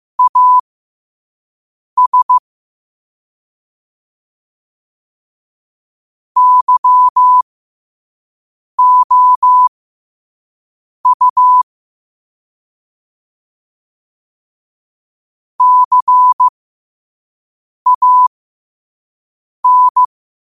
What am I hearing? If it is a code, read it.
Morse code, heard as a keyed tone, about 15 wpm, spaced as for 5 wpm, 1000 Hz. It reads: AS YOU CAN
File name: qotdfiles / 5wpm